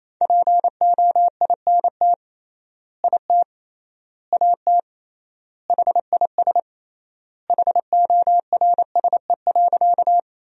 Morse code, heard as a keyed tone, about 28 wpm, 700 Hz. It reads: POINT IT AT HIS HORSE.
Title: 3x Word-Spacing - Sentences from Top 300 Words - 28wpm